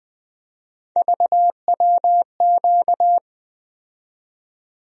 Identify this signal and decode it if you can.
Morse code, heard as a keyed tone, about 20 wpm, 700 Hz. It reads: VWQ
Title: USA Call Sign Postfixes - 3 Letters 20wpm